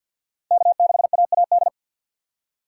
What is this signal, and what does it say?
Morse code, heard as a keyed tone, about 52 wpm, 700 Hz. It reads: K6AAD
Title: USA Call Signs - 5 Characters - 2x Character Spacing - 50wpm